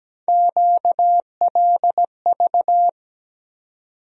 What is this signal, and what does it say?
Morse code, heard as a keyed tone, about 17 wpm, 700 Hz. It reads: QLV
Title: USA Call Sign Postfixes - 3 Letters 17wpm